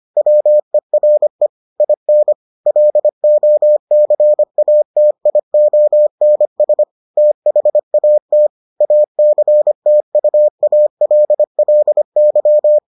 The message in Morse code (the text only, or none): WERE IN LOCATIONS THAT ACTUALLY